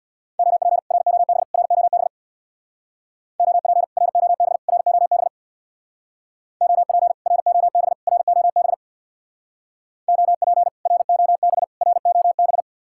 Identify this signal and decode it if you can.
Morse code, heard as a keyed tone, about 65 wpm, 700 Hz. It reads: OP ROB ROB OP ROB ROB OP ROB ROB OP ROB ROB